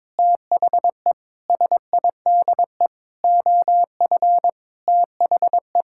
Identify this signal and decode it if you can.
Morse code, heard as a keyed tone, about 22 wpm, 700 Hz. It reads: THE SIDE OF THE
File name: practice / 22wpm / Sentences with 8-Word Phrases - 22wpm